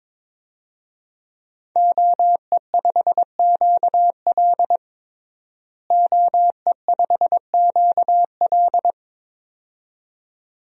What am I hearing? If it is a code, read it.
Morse code, heard as a keyed tone, about 22 wpm, 700 Hz. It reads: OE5QL OE5QL
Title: DXCC-100 Call Signs - 5 Characters - Repeated 2x - 22wpm